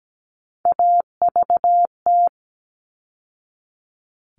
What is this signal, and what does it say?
Morse code, heard as a keyed tone, about 17 wpm, 700 Hz. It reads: AVT
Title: All Cut Numbers - 3-Digit 17wpm